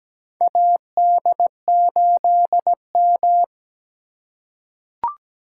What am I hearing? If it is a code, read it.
Morse code, heard as a keyed tone, about 17 wpm, 700 Hz. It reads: AD8M
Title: USA Call Signs - 4 Characters - 3x Character Spacing - 17wpm